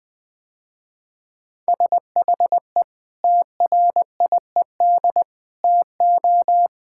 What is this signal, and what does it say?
Morse code, heard as a keyed tone, about 20 wpm, 700 Hz. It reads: SHE TRIED TO
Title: Sentences with 3-Word Phrases - Spoken Beginning - 20wpm